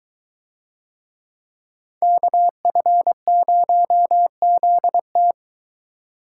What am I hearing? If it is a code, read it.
Morse code, heard as a keyed tone, about 22 wpm, 700 Hz. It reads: KF0ZT